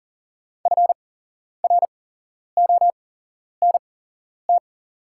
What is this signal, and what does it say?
Morse code, heard as a keyed tone, about 40 wpm, spaced as for 11 wpm, 690 Hz. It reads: FRONT
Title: 8x Character Spacing - Top 500 Words - 40wpm